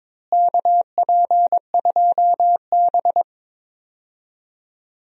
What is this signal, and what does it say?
Morse code, heard as a keyed tone, about 21 wpm, 700 Hz. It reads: KP2B
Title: DXCC-100 Call Signs - 4 Characters - Repeated 3x - 22wpm